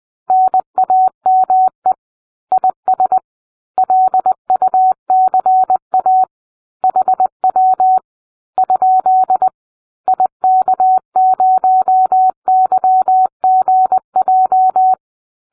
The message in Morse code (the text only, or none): NAME IS LUCA HW ? IK0YGJ